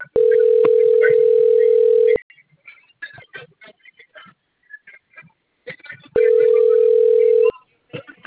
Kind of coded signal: telephone call progress tone

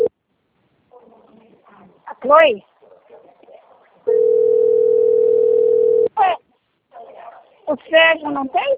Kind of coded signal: telephone call progress tone

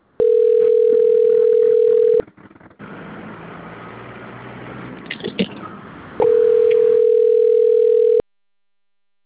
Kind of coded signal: telephone call progress tone

